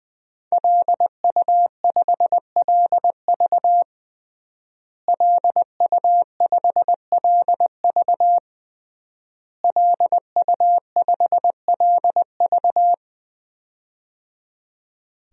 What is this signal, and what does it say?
Morse code, heard as a keyed tone, about 20 wpm, 700 Hz. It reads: LU5LV LU5LV LU5LV
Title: DXCC-100 Call Signs - 5 Characters - Repeated 3x - 20wpm